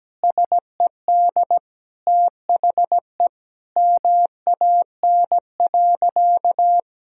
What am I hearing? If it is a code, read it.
Morse code, heard as a keyed tone, about 17 wpm, 700 Hz. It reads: SED THE MAN.
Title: Sentences from Top 200 Words and Abbreviations 17wpm